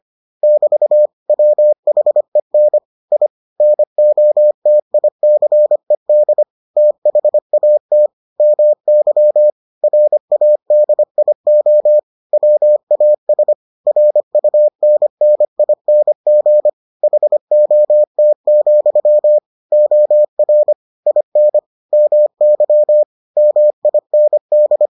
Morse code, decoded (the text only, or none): <BT> WHEN I NOTICED THAT MY RADIO WAS RUNNING HOT, OR IN MY MIND